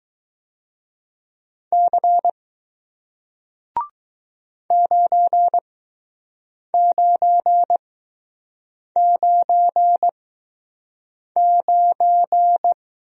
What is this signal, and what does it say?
Morse code, heard as a keyed tone, about 20 wpm, spaced as for 6 wpm, 700 Hz. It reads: C 9999